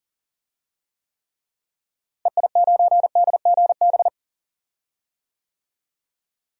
Morse code, heard as a keyed tone, about 37 wpm, 700 Hz. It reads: EI9DGB
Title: DXCC-100 Call Signs - 6 Characters - Repeated 2x - 40wpm